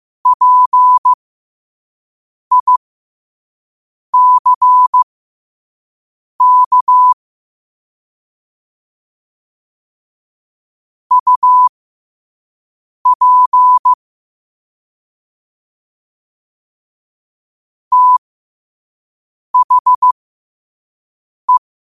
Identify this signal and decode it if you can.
Morse code, heard as a keyed tone, about 15 wpm, spaced as for 5 wpm, 1000 Hz. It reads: PICK UP THE
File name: qotdfiles / 5wpm